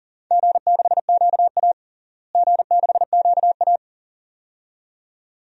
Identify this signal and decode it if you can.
Morse code, heard as a keyed tone, about 37 wpm, 700 Hz. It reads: G6QA G6QA
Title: DXCC-100 Call Signs - 4 Characters - Repeated 2x - 40wpm